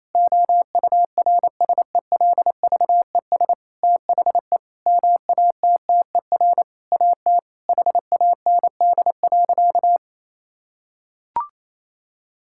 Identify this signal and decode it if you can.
Morse code, heard as a keyed tone, about 28 wpm, 700 Hz. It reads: OURSELVES THE MATTER AT HAND.
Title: Sentences from Top 1600 Words 28wpm